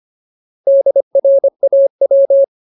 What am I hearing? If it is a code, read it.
Morse code, heard as a keyed tone, about 25 wpm, 550 Hz. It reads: DRAW
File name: Ultimate Word List / 25WPM Ultimate